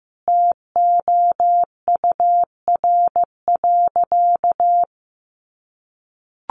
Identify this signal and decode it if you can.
Morse code, heard as a keyed tone, about 15 wpm, 700 Hz. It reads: TOUR.
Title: Sentences from Top 1400 Words 15wpm